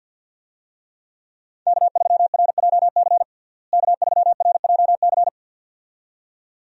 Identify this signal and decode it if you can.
Morse code, heard as a keyed tone, about 50 wpm, 700 Hz. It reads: K3RJC K3RJC
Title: USA Call Signs - 5 Characters - Repeated 2x - 50wpm